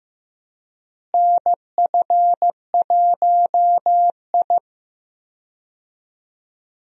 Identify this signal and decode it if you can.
Morse code, heard as a keyed tone, about 15 wpm, 700 Hz. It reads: NF1I
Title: USA Call Signs - 4 Characters - Repeated 2x - 15wpm